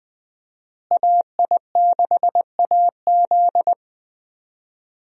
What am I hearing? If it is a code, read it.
Morse code, heard as a keyed tone, about 20 wpm, 700 Hz. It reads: AI6AZ
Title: USA Call Signs - 5 Characters - Repeated 2x - 20wpm